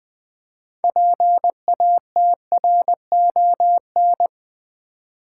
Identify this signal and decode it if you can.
Morse code, heard as a keyed tone, about 20 wpm, 700 Hz. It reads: PATRON
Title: Build A Word - 3 Letters At A Time - 6-Letter Words - 20wpm